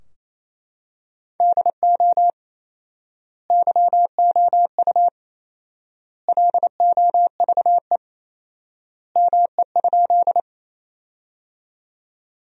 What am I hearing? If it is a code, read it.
Morse code, heard as a keyed tone, about 28 wpm, 700 Hz. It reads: DO YOU LOVE ME?
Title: 4x Word-Spacing - Sentences from Top 400 Words - 28wpm